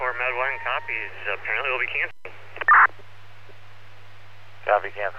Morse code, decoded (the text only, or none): none